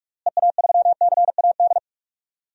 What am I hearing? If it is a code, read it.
Morse code, heard as a keyed tone, about 46 wpm, 700 Hz. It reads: EA3CAD
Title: DXCC-100 Call Signs - 6 Characters - Repeated 3x - 45wpm